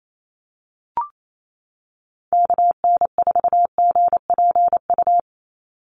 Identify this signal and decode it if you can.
Morse code, heard as a keyed tone, about 28 wpm, 700 Hz. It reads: KN4GPU
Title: USA Call Signs 28wpm